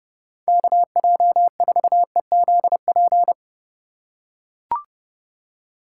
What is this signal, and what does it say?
Morse code, heard as a keyed tone, about 32 wpm, 700 Hz. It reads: KJ4EZP
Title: USA Call Signs - 6 Characters - 2x Character Spacing - 30wpm